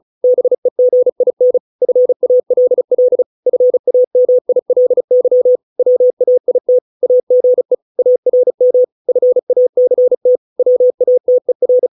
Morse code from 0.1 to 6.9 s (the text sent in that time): BEGIN FALL FAMILY WAIT